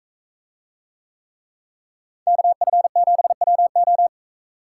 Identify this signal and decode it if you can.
Morse code, heard as a keyed tone, about 42 wpm, 700 Hz. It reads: KF7WO